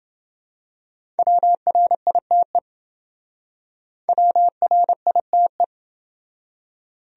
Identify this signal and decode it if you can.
Morse code, heard as a keyed tone, about 27 wpm, 700 Hz. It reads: WRITE WRITE